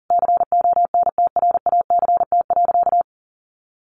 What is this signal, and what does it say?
Morse code, heard as a keyed tone, about 40 wpm, 700 Hz. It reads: CONTRACT.
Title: Sentences from Top 1400 Words 40wpm